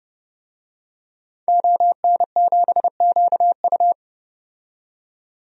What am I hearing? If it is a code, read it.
Morse code, heard as a keyed tone, about 32 wpm, 700 Hz. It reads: ON7QU
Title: DXCC-100 Call Signs - 5 Characters - 30wpm